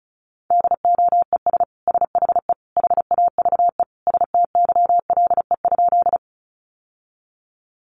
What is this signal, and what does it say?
Morse code, heard as a keyed tone, about 34 wpm, 700 Hz. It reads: DOES SHE HAVE STYLE?
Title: Sentences from Top 800 Words 35wpm